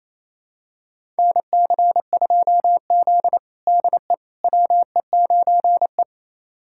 Morse code, heard as a keyed tone, about 28 wpm, 700 Hz. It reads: NC2Z DE WE9E
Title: USA Callsign DE Callsign - 4 Characters - 28wpm